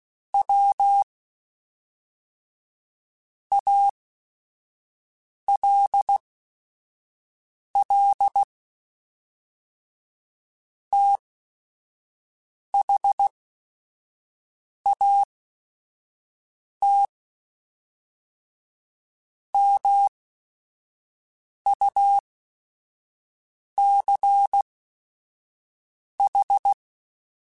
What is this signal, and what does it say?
Morse code, heard as a keyed tone, about 16 wpm, spaced as for 5 wpm, 800 Hz. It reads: W ALL THAT MUCH
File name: QSOs / QSOs-mp3s / WD0BGP 2@5